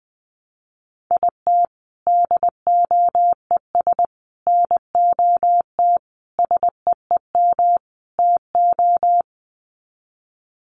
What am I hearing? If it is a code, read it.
Morse code, heard as a keyed tone, about 20 wpm, 700 Hz. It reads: IT DOES NOT SEEM TO
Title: Sets of 5 Words Encore2 20wpm